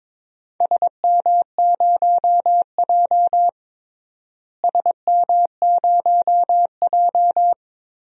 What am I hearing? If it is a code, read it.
Morse code, heard as a keyed tone, about 22 wpm, 700 Hz. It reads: SM0J SM0J
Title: DXCC-100 Call Signs - 4 Characters - Repeated 3x - 22wpm